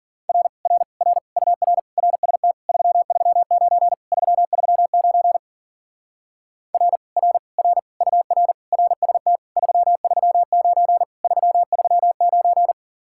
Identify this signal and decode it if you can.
Morse code, heard as a keyed tone, about 48 wpm, 700 Hz. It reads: R R R UR RST 339 339 R R R UR RST 339 339
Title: Speed-Racing Common QSO Phrases - 40wpm